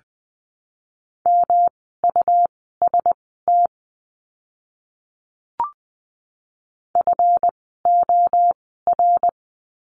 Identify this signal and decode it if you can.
Morse code, heard as a keyed tone, about 20 wpm, spaced as for 15 wpm, 700 Hz. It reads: MUST FOR
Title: 237 Lesson 38f - Letter q - Words 20wpm